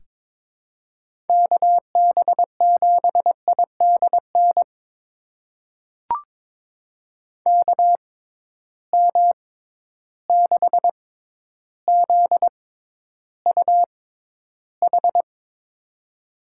Morse code, heard as a keyed tone, about 22 wpm, spaced as for 11 wpm, 700 Hz. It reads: KB7IDN KM6ZUH
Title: USA Call Signs - 6 Characters - 6x Character Spacing - 22wpm